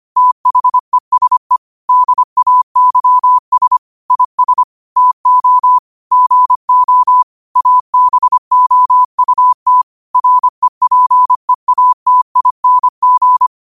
Morse code, heard as a keyed tone, about 25 wpm, 1000 Hz. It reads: THESE DAYS IS TO GO ABOUT REPEATING
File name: qotdfiles / 25wpm